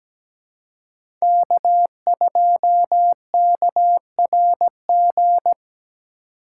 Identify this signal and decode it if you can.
Morse code, heard as a keyed tone, about 17 wpm, 700 Hz. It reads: K2KRG
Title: USA Call Signs - 2x Character Spacing - 17wpm